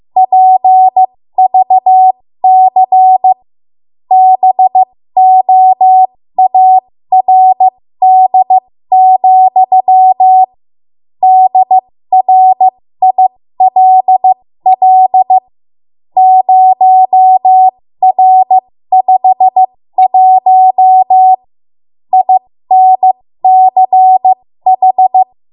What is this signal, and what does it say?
Morse code, heard as a keyed tone, about 15 wpm, spaced as for 13 wpm, 750 Hz. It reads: PVC BOARD, DRILL 0R51 INCH
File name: Morse / 13WPM